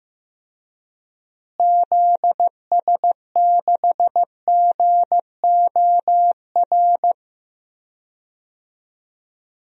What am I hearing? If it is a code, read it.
Morse code, heard as a keyed tone, about 15 wpm, 700 Hz. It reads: ZS6GOR
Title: DXCC-100 Call Signs - 6 Characters - Repeated 3x - 15wpm